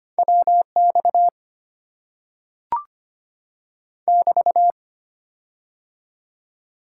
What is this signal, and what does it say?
Morse code, heard as a keyed tone, about 25 wpm, 700 Hz. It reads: WX <BT>
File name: Top 90 QSO Elements - 6x Character Spacing - 25wpm